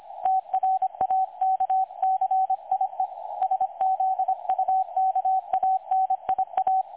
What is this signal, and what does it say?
Morse code, heard as a keyed tone, about 25 wpm, 760 Hz, with radio noise. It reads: TRAKCIE SZUKANIA
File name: websdr recording 2018-03-25T20 27 14Z 3565.0kHz